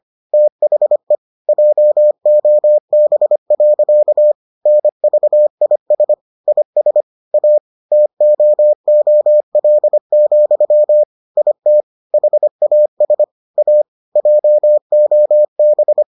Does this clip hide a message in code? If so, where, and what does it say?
Yes: Morse code from 0.3 to 16.0 s, heard as a keyed tone, about 25 wpm, 600 Hz. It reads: THE JOB. NVIS IS A TOOL, IT HAS A JOB